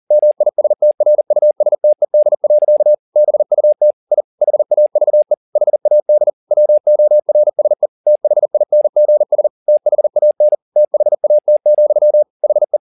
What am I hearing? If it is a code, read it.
Morse code, heard as a keyed tone, about 40 wpm, 600 Hz. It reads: MISTRUSTED. BUT I HAVE HAD WORSE THINGS THAN THAT, HE
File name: the old man and the sea 2 40wpm